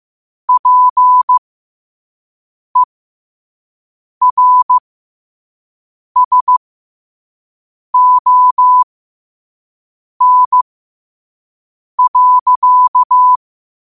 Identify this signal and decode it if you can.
Morse code, heard as a keyed tone, about 15 wpm, spaced as for 5 wpm, 1000 Hz. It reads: PERSON.
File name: qotdfiles / 5wpm